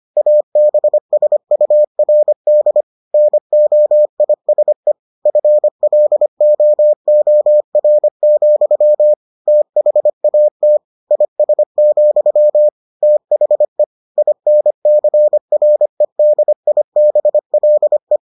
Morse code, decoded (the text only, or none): ABSURD NOISE FLOOR, THAT IS, THE INCREDIBLE